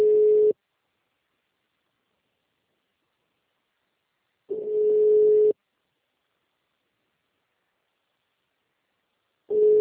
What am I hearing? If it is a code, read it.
Ringback tone (Europe, 425 Hz).